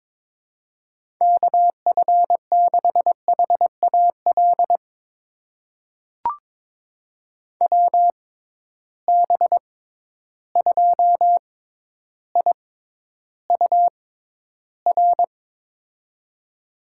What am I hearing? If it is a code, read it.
Morse code, heard as a keyed tone, about 22 wpm, spaced as for 11 wpm, 700 Hz. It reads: KF6HAL WB2IUR
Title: USA Call Signs - 6 Characters - 6x Character Spacing - 22wpm